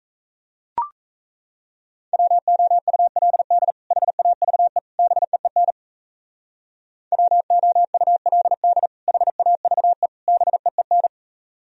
Morse code, heard as a keyed tone, about 33 wpm, 700 Hz. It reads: WOULD HAVE BEEN WOULD HAVE BEEN
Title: Speed-Racing - Sets of 3 Words 28wpm